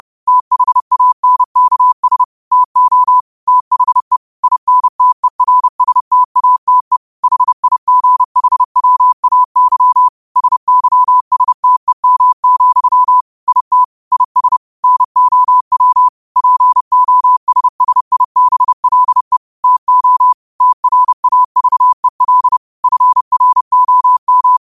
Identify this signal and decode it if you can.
Morse code, heard as a keyed tone, about 30 wpm, 1000 Hz. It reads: THANKS TO THE INTERSTATE HIGHWAY SYSTEM, IT IS NOW POSSIBLE TO TRAVEL FROM
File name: qotdfiles / 30wpm